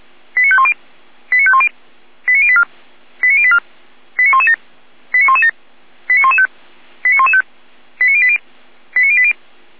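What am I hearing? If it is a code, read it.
Five-tone selective call (ZVEI), heard as quick runs of five tones, tones of about 70 ms. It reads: 89510 89510 89084 89084 89117 89117 89115 89115 89080 89080
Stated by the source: SDR-received ZVEI five-tone call 89094.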